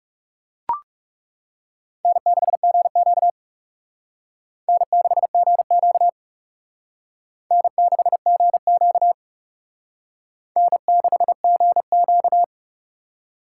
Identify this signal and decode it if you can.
Morse code, heard as a keyed tone, about 37 wpm, 700 Hz. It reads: N6GQ N6GQ N6GQ N6GQ
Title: Speed Racing - USA 1x2 Call Signs 30wpm